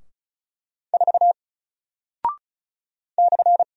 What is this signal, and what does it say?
Morse code, heard as a keyed tone, about 34 wpm, 700 Hz. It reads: 4 /
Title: Single Character (Letter:Number:Punctuation:Prosign) Alt 35wpm